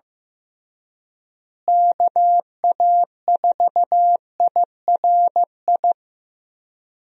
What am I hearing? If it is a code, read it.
Morse code, heard as a keyed tone, about 15 wpm, 700 Hz. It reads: KA4IRI
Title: USA Call Signs - 6 Characters - Repeated 2x - 15wpm